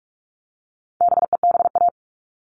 Morse code, heard as a keyed tone, about 44 wpm, 700 Hz. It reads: BEBA
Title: All Cut Numbers - 4-Digit 45wpm